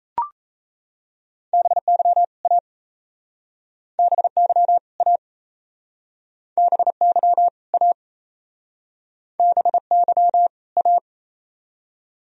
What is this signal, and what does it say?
Morse code, heard as a keyed tone, about 37 wpm, 700 Hz. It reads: BY A BY A BY A BY A